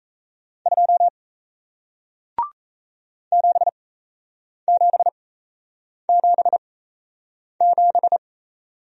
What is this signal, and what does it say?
Morse code, heard as a keyed tone, about 38 wpm, spaced as for 8 wpm, 700 Hz. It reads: 2 7777